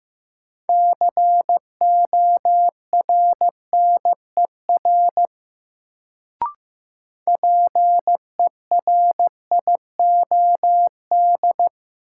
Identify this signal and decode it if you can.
Morse code, heard as a keyed tone, about 15 wpm, 700 Hz. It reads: CORNER PERIOD
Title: Sets of 5 Words - New Word Drill - 15wpm